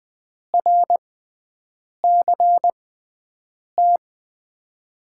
Morse code, heard as a keyed tone, about 20 wpm, spaced as for 7 wpm, 700 Hz. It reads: RCT
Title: USA Call Signs - 6 Characters - 6x Character Spacing - 20wpm